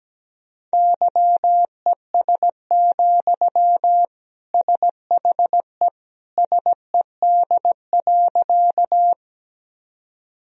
Morse code, heard as a keyed tone, about 17 wpm, 700 Hz. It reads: YES, SHE SED.
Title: Sentences from Top 500 Words and Abbreviations 17wpm